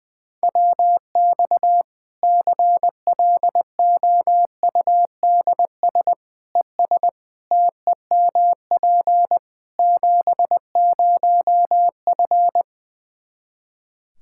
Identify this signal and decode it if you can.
Morse code, heard as a keyed tone, about 20 wpm, 700 Hz. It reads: WX CLOUDS ES TEMP 70F
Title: Practice QSO - Exchange 2 - Info - 20wpm